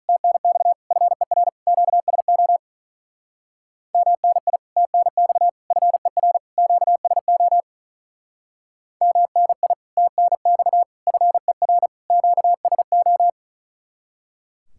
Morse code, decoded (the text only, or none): TNX FER QSO MNI TNX FER QSO MNI TNX FER QSO